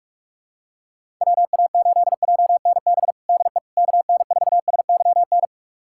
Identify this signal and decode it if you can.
Morse code, heard as a keyed tone, about 45 wpm, 700 Hz. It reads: WA8JNB DE KN4SYN
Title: USA Callsign DE Callsign - 6 Characters - 45wpm